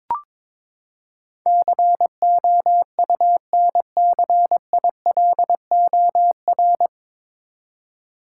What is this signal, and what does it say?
Morse code, heard as a keyed tone, about 22 wpm, 700 Hz. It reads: COUNCILOR
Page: Morse Code Ninja practice 22wpm